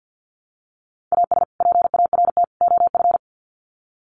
Morse code, heard as a keyed tone, about 50 wpm, 700 Hz. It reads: AS PART OF
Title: Sets of 3 Words 50wpm